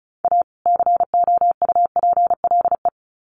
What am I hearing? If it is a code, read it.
Morse code, heard as a keyed tone, about 34 wpm, 700 Hz. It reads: A COUPLE
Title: Sets of 2 Words 35wpm